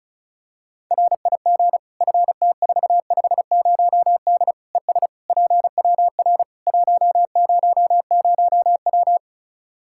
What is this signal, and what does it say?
Morse code, heard as a keyed tone, about 36 wpm, 700 Hz. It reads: RIG FT450D ES PWR 100W
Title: Practice QSO - Exchange 2 - Info - 35wpm